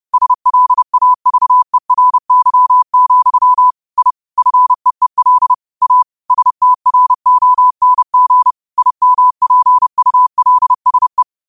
Morse code, heard as a keyed tone, about 30 wpm, 1000 Hz. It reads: SLAVERY, I FEEL A STRONG IMPULSE
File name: qotdfiles / 30wpm